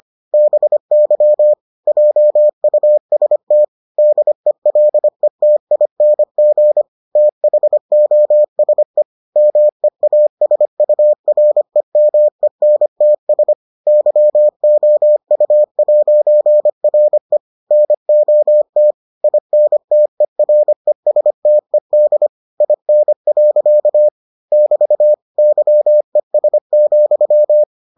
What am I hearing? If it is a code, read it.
Morse code, heard as a keyed tone, about 25 wpm, 600 Hz. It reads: BY JUST DELETING THOSE MEASUREMENTS YOU'RE NOT INTERESTED IN. <BT> YES,